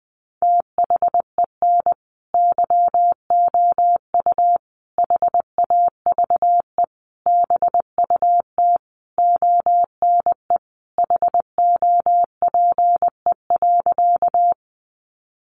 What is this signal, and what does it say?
Morse code, heard as a keyed tone, about 20 wpm, 700 Hz. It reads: THEN YOU HAVE BUT ONE HOPE.
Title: Sentences from Top 1000 Words 20wpm